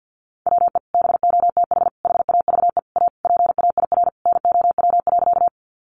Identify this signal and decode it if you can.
Morse code, heard as a keyed tone, about 50 wpm, 700 Hz. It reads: WE BOTH HAVE A PAIR NOW.
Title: Sentences from Top 1200 Words 50wpm